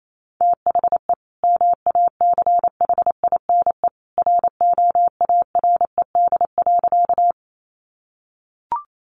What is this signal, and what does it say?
Morse code, heard as a keyed tone, about 28 wpm, 700 Hz. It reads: THE MACHINE ROARED.
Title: A Sound Of Thunder 28wpm